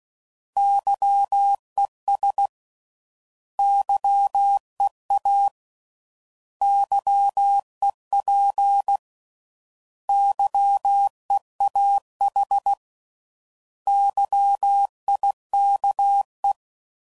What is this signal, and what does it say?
Morse code, heard as a keyed tone, about 16 wpm, 800 Hz. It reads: YES YEA YEP YEAH YIKE